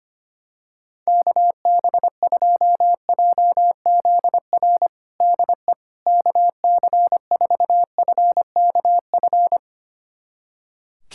Morse code, heard as a keyed tone, about 25 wpm, 700 Hz. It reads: KB2JZR DE KC4FKF
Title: USA Callsign DE Callsign - 6 Characters - 25wpm